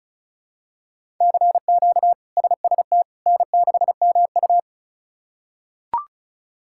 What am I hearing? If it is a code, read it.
Morse code, heard as a keyed tone, about 34 wpm, 700 Hz. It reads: CQ SST N6MU
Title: K1USN SST QSO Practice - 35wpm